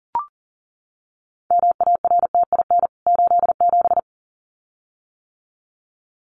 Morse code, heard as a keyed tone, about 40 wpm, 700 Hz. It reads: MARTIN 87
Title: CWT Exchange 40wpm